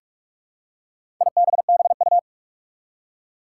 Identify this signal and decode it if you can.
Morse code, heard as a keyed tone, about 46 wpm, 700 Hz. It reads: IBBU
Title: Four Challenging Letter-Number Combinations 45wpm